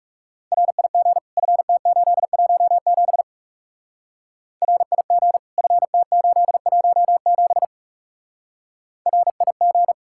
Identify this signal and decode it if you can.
Morse code, heard as a keyed tone, about 35 wpm, 700 Hz. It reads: RIG FT817 RIG FT817 RIG
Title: Speed-Racing Common QSO Phrases - 30wpm